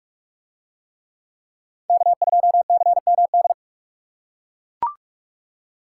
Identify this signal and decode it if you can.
Morse code, heard as a keyed tone, about 44 wpm, 700 Hz. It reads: K2CMD